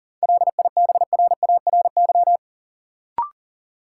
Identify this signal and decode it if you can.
Morse code, heard as a keyed tone, about 40 wpm, 700 Hz. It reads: LIBRARY
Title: Sentences from Top 600 Words - New Word Drill - 40wpm